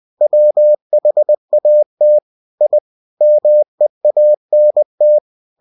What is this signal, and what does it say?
Morse code, heard as a keyed tone, about 20 wpm, 600 Hz. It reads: WHAT I MEANT
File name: cw-books / winnie-the-pooh / winnie-20wpm-page-0003